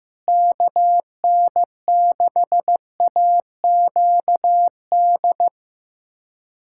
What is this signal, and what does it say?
Morse code, heard as a keyed tone, about 16 wpm, 700 Hz. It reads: KN6AQD
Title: USA Callsign DE Callsign - 6 Characters - 15wpm